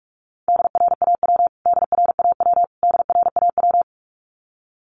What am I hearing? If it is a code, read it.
Morse code, heard as a keyed tone, about 44 wpm, 700 Hz. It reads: DRAW DRAW DRAW
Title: Top 500 Words Repeated 45wpm